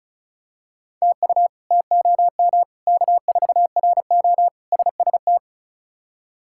Morse code, heard as a keyed tone, about 36 wpm, 700 Hz. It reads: TU TOM K4RO SST
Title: K1USN SST QSO Practice - 35wpm